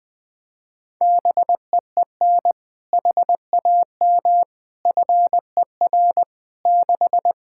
Morse code, heard as a keyed tone, about 20 wpm, 700 Hz. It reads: BEEN HAM FER 6
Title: K1USN SST QSO Practice 20wpm